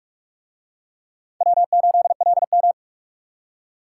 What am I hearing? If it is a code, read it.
Morse code, heard as a keyed tone, about 41 wpm, 700 Hz. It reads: W8LM